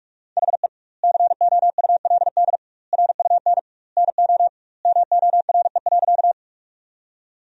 Morse code, heard as a keyed tone, about 46 wpm, 700 Hz. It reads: HE COULD RUN NO MORE.
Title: 2x Word-Spacing - Sentences from Top 300 Words - 45wpm